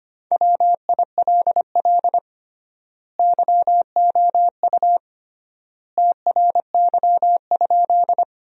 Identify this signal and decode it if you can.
Morse code, heard as a keyed tone, about 25 wpm, 700 Hz. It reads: WILL YOU TRY?
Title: 3x Word-Spacing - Sentences from Top 200 Words - 25wpm